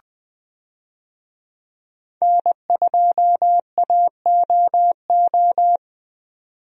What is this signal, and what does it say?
Morse code, heard as a keyed tone, about 20 wpm, 700 Hz. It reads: N2AOO
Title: USA Call Signs - 5 Characters - 20wpm